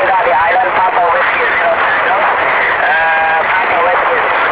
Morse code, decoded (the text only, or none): none